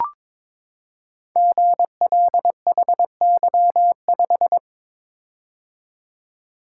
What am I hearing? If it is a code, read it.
Morse code, heard as a keyed tone, about 22 wpm, 700 Hz. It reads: GLHY5
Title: Code Groups - 5 Letter:Numbers - 22wpm